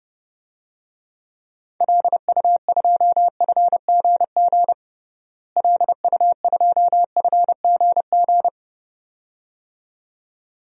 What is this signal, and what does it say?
Morse code, heard as a keyed tone, about 30 wpm, 700 Hz. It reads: LU2FGG LU2FGG
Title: DXCC-100 Call Signs - 6 Characters - Repeated 2x - 30wpm